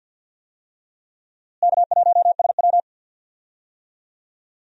Morse code, heard as a keyed tone, about 49 wpm, 700 Hz. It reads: K1SW
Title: USA Call Signs - 4 Characters - Repeated 3x - 50wpm